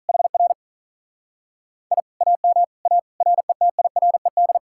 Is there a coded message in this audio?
Yes: Morse code from 0.1 to 4.6 s, heard as a keyed tone, about 44 wpm, 700 Hz. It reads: HR I AM A RETIRED